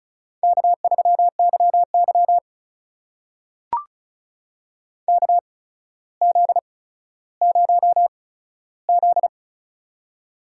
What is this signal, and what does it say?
Morse code, heard as a keyed tone, about 36 wpm, spaced as for 15 wpm, 700 Hz. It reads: K3YY KZ0Z
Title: USA Call Signs - 4 Characters - 8x Character Spacing - 35wpm